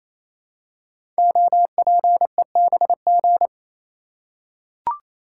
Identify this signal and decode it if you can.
Morse code, heard as a keyed tone, about 28 wpm, 700 Hz. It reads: OPEBG
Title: Code Groups - 5 Letter:Numbers - 28wpm